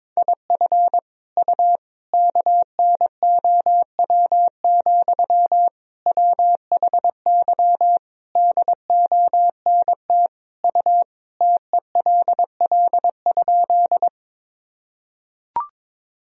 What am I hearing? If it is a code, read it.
Morse code, heard as a keyed tone, about 22 wpm, 700 Hz. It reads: IF U KNOW, WHY DONT U TELL?
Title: Sentences from Top 300 Words and Abbreviations 22wpm